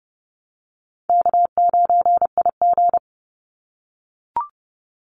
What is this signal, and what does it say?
Morse code, heard as a keyed tone, about 30 wpm, 700 Hz. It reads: K9IG
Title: Call Signs - 30wpm